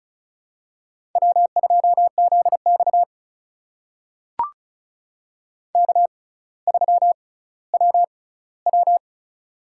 Morse code, heard as a keyed tone, about 36 wpm, spaced as for 18 wpm, 700 Hz. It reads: W2ZX K3WW
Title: USA Call Signs - 4 Characters - 6x Character Spacing - 35wpm